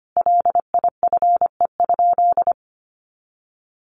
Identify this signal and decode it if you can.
Morse code, heard as a keyed tone, about 25 wpm, 700 Hz. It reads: LIFE?
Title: Sentences from Top 300 Words 25wpm